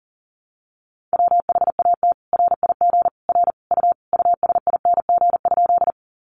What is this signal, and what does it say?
Morse code, heard as a keyed tone, about 40 wpm, 700 Hz. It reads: WHAT RIG R U USING?
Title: Common QSO Phrases 40wpm